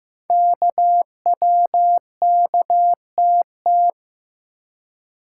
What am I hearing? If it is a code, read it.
Morse code, heard as a keyed tone, about 15 wpm, 700 Hz. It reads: KWKTT
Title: Code Groups - 5 Letters - 15wpm